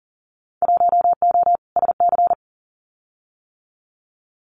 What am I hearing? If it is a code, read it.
Morse code, heard as a keyed tone, about 40 wpm, 700 Hz. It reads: 1O SC
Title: Winter Field Day Exchanges 40wpm